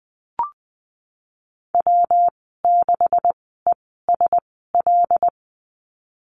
Morse code, heard as a keyed tone, about 20 wpm, spaced as for 15 wpm, 700 Hz. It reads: W6ESL
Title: 128 Lesson 21g - Number 6 - Call Signs 20wpm